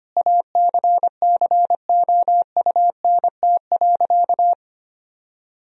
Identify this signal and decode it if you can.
Morse code, heard as a keyed tone, about 25 wpm, 700 Hz. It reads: ACCOUNT.
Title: Sentences from Top 1000 Words - Review Part 4 - 25wpm